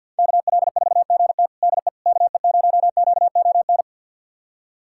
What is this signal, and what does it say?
Morse code, heard as a keyed tone, about 51 wpm, 700 Hz. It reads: KL4GT DE KE0QON
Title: USA Callsign DE Callsign 50wpm